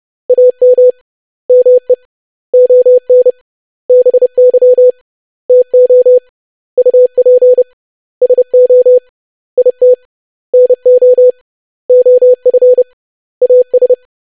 Morse code, heard as a keyed tone, about 30 wpm, 500 Hz. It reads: AM ME ON BY TO UP SO IT NO OF AS
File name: lesson / web / basic100-25wpm